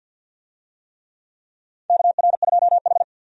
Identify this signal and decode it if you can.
Morse code, heard as a keyed tone, about 47 wpm, 700 Hz. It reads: KR2H
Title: USA Call Signs - 4 Characters - 50wpm